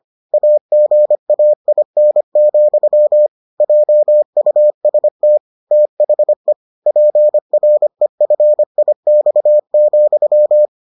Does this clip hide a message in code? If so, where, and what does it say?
Yes: Morse code from 0.3 to 10.7 s, heard as a keyed tone, about 25 wpm, 600 Hz. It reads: AGAIN, JUST THE PREFIX,